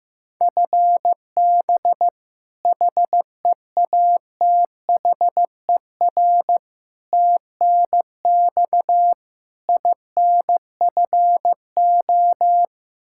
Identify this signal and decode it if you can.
Morse code, heard as a keyed tone, about 15 wpm, 700 Hz. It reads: FB HEATHER TNX INFO
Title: Practice QSO - Exchange 3 - More Info - 15wpm